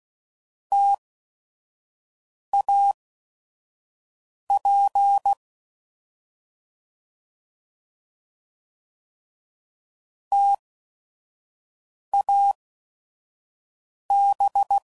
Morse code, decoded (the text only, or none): TAP TAB